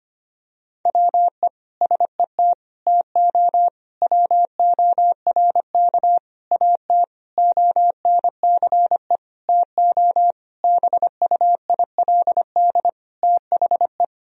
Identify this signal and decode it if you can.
Morse code, heard as a keyed tone, about 25 wpm, 700 Hz. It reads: WE SET TO WORK AT ONCE TO BUILD THE
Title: Sentences from Top 1000 Words - Review Part 1 - 25wpm